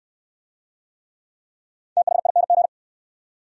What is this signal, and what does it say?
Morse code, heard as a keyed tone, about 59 wpm, 700 Hz. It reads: THING